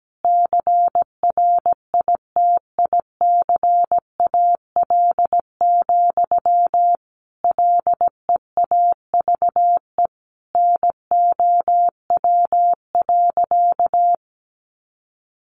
Morse code, heard as a keyed tone, about 17 wpm, 700 Hz. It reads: CRITICAL, LEAVE NOW.
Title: Sentences from Top 1100 Words 17wpm